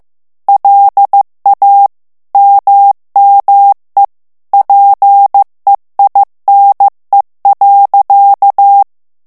Morse code, heard as a keyed tone, about 15 wpm, 800 Hz. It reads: LA MME PEINE.
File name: morse mp3 / FRANCAIS TEXTE CLAIR 2 VITESSE 14 MOTS MINUTES